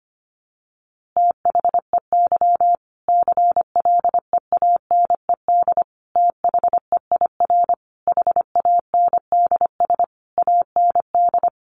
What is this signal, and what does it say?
Morse code, heard as a keyed tone, about 25 wpm, 700 Hz. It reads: THEY CLEANED THEIR HANDS AND
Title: A Sound Of Thunder 25wpm